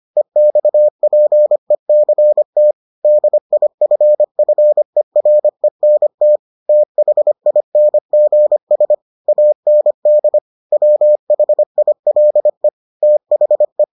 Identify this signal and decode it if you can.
Morse code, heard as a keyed tone, about 25 wpm, 600 Hz. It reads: EXPECT DIFFERENT THINGS AND WHILE THE